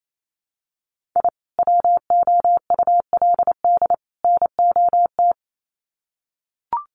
Morse code, heard as a keyed tone, about 28 wpm, 700 Hz. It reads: I WOULD NOT
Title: Sets of 3 Words 28wpm